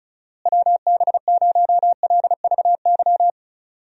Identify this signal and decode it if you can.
Morse code, heard as a keyed tone, about 36 wpm, 700 Hz. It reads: WB0LVY
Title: USA Call Signs - 6 Characters - Repeated 2x - 35wpm